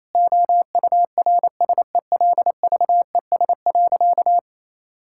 Morse code, heard as a keyed tone, about 28 wpm, 700 Hz. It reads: OURSELVES.
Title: Sentences from Top 1600 Words 28wpm